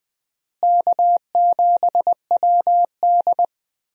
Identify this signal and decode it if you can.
Morse code, heard as a keyed tone, about 20 wpm, 700 Hz. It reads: K7WD
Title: USA Call Signs - 4 Characters - Repeated 3x - 20wpm